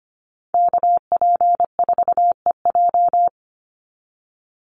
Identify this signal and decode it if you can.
Morse code, heard as a keyed tone, about 25 wpm, 700 Hz. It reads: KP4EJ
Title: Call Signs - 25wpm